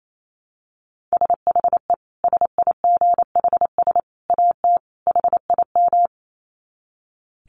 Morse code, heard as a keyed tone, about 28 wpm, 700 Hz. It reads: SHE SIGHS AT HIM
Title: Dit Challenge 28wpm